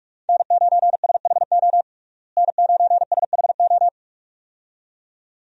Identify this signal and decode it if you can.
Morse code, heard as a keyed tone, about 41 wpm, 700 Hz. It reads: N9SHO N9SHO
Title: USA Call Signs - 5 Characters - Repeated 2x - 45wpm